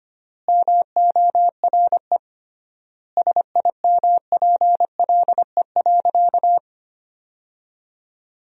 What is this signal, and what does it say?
Morse code, heard as a keyed tone, about 25 wpm, 700 Hz. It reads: MORE SIMPLE.
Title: 3x Word-Spacing - Sentences from Top 500 Words - 25wpm